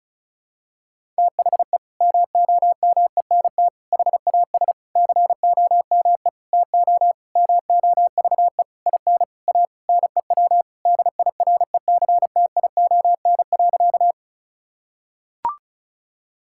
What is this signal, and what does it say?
Morse code, heard as a keyed tone, about 36 wpm, 700 Hz. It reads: THE MOMENT HAS COME TO MOVE IN A NEW DIRECTION.
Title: Sentences from Top 1000 Words - Review Part 5 - 35wpm